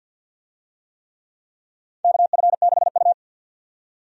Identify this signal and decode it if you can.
Morse code, heard as a keyed tone, about 52 wpm, 700 Hz. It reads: KF6U
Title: USA Call Signs - 4 Characters - 50wpm